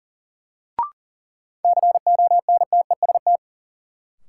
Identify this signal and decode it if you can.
Morse code, heard as a keyed tone, about 40 wpm, 700 Hz. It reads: CONTEST
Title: Sentences from Top 1600 Words - New Word Drill Alt - 40wpm